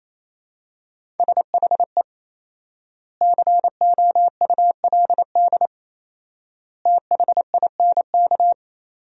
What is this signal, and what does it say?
Morse code, heard as a keyed tone, about 28 wpm, 700 Hz. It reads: SHE COULD THINK
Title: 4x Word-Spacing - Sentences from Top 500 Words - 28wpm